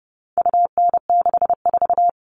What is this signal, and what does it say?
Morse code, heard as a keyed tone, about 30 wpm, 700 Hz. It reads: UN64
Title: All Cut Numbers - 4-Digit 30wpm